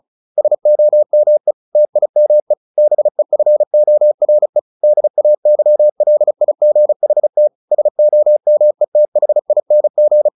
Morse code, heard as a keyed tone, about 37 wpm, 600 Hz. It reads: SOME TIME BEFORE DAYLIGHT SOMETHING
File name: the old man and the sea / the old man and the sea 1 35wpm